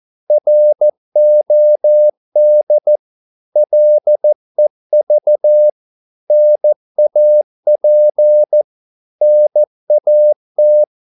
Morse code, heard as a keyed tone, about 14 wpm, 600 Hz. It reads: ROD LEV NAP NAT
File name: Korte ord DK 14wpm